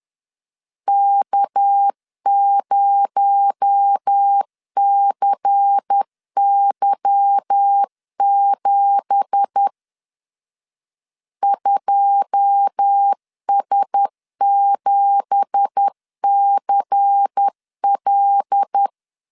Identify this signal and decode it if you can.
Morse code, heard as a keyed tone, about 10 wpm, 800 Hz. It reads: K0CY7 2S7CL
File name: LEZIONI / 9 2867 10wpm 2733